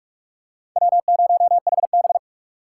Morse code, heard as a keyed tone, about 43 wpm, 700 Hz. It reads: W0HB
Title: USA Call Signs - 4 Characters - 3x Character Spacing - 45wpm